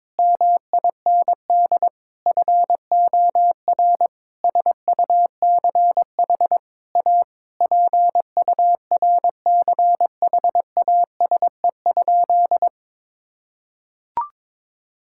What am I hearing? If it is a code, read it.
Morse code, heard as a keyed tone, about 22 wpm, 700 Hz. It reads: MIND FOR SUCH A PURCHASE?
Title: Sentences from Top 1000 Words - Review Part 6 - 22wpm